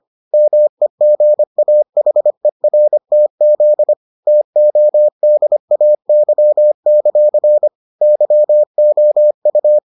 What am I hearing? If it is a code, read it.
Morse code, heard as a keyed tone, about 25 wpm, 600 Hz. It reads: MEGAHERTZ TODAY; YOU